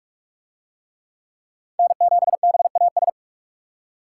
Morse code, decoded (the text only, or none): N7BAS